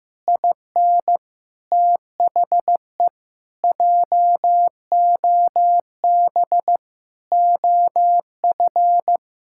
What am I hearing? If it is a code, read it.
Morse code, heard as a keyed tone, about 15 wpm, 700 Hz. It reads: IN THE JOB OF
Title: Pangram Challenge v2 - 15wpm